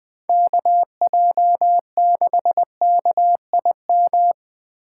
Morse code, heard as a keyed tone, about 24 wpm, 700 Hz. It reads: KJ6KIM
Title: Speed Racing - USA 2x3 Call Signs 20wpm